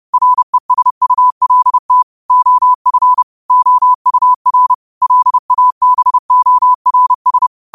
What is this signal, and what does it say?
Morse code, heard as a keyed tone, about 30 wpm, 1000 Hz. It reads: RESULT OF OUR LABORS
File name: qotdfiles / 30wpm